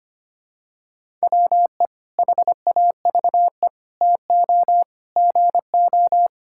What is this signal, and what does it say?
Morse code, heard as a keyed tone, about 25 wpm, 700 Hz. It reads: WE HAVE TO GO
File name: practice / 25wpm / Sentences with 3-Word Phrases - 25wpm